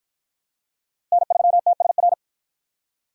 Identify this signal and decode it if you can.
Morse code, heard as a keyed tone, about 53 wpm, 700 Hz. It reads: N3TSR